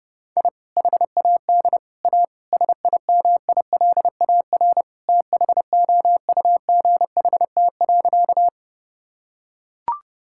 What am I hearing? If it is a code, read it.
Morse code, heard as a keyed tone, about 32 wpm, 700 Hz. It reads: I HAD A SIMILAR THOUGHT.
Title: Sentences from Top 1000 Words - Review Part 5 - 30wpm